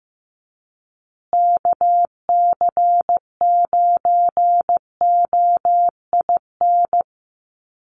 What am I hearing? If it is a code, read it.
Morse code, heard as a keyed tone, about 15 wpm, 700 Hz. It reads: KC9OIN
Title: USA Call Signs Repeated 2x 15wpm